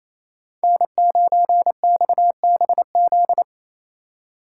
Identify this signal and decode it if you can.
Morse code, heard as a keyed tone, about 28 wpm, 700 Hz. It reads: N9XBZ
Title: USA Call Signs - 5 Characters - 6x Character Spacing - 28wpm